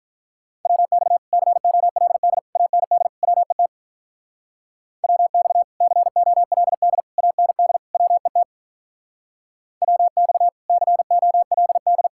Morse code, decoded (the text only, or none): WX COLD AND WET WX COLD AND WET WX COLD